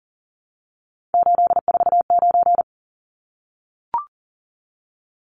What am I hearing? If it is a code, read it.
Morse code, heard as a keyed tone, about 40 wpm, 700 Hz. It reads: 849
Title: Numbers - 3-Digit 40wpm